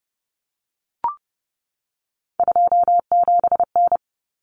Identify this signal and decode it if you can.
Morse code, heard as a keyed tone, about 30 wpm, 700 Hz. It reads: 27N
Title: Common Cut Numbers - 3-Digit 30wpm